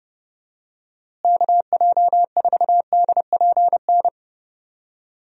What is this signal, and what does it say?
Morse code, heard as a keyed tone, about 30 wpm, 700 Hz. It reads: KJ4DPN
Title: USA Call Signs - 6 Characters - Repeated 3x - 30wpm